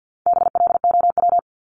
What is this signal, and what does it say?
Morse code, heard as a keyed tone, about 50 wpm, 700 Hz. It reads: BLOW
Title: Top 1751-2000 Words 50wpm